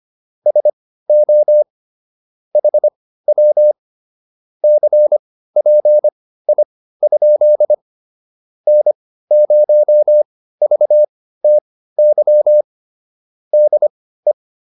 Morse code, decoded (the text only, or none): SO HW CPI? N0VTY DE